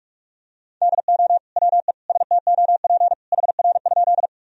Morse code, heard as a keyed tone, about 46 wpm, 700 Hz. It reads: DO WE STOP HR?
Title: Sentences from Top 300 Words and Abbreviations 45wpm